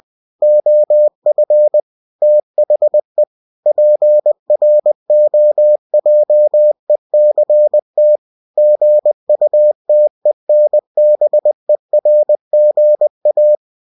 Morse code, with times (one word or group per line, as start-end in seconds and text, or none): 0.4-1.8 OF
2.2-3.2 THE
3.7-8.2 PROJECT
8.6-13.6 GUTENBERGA